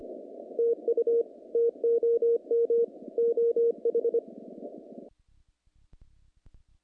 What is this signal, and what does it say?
Morse code, heard as a keyed tone, about 25 wpm, 470 Hz. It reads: TU TOM OH
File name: contests / naqp-cw-k5tr-op-k5pi- -2200